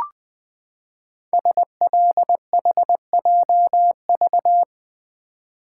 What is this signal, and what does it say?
Morse code, heard as a keyed tone, about 20 wpm, 700 Hz. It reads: SLHJV
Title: Code Groups - 5 Letters - 20wpm